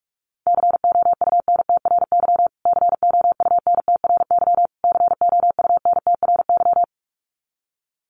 Morse code, heard as a keyed tone, about 44 wpm, 700 Hz. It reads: COUNTRY COUNTRY COUNTRY
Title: Top 500 Words Repeated 45wpm